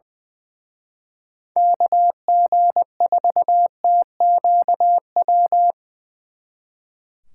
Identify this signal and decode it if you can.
Morse code, heard as a keyed tone, about 24 wpm, 700 Hz. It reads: KG4TQW
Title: Speed Racing - USA 2x3 Call Signs 20wpm